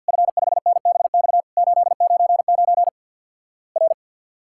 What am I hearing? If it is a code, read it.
Morse code, heard as a keyed tone, about 50 wpm, 700 Hz. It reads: F5NBX 899 R
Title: Worked All Europe DX Contest - QTC Exchange with Repeats - 50wpm